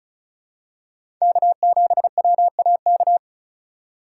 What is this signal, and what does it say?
Morse code, heard as a keyed tone, about 36 wpm, 700 Hz. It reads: K7WAK
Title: USA Call Signs - 5 Characters - 3x Character Spacing - 35wpm